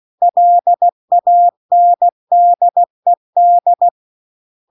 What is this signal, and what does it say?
Morse code, heard as a keyed tone, about 16 wpm, 700 Hz. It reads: LANDED
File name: 16WPM